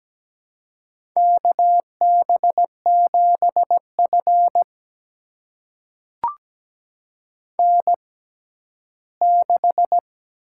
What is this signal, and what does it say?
Morse code, heard as a keyed tone, about 17 wpm, spaced as for 12 wpm, 700 Hz. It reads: KB7F N6
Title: USA Call Signs - 4 Characters - 6x Character Spacing - 17wpm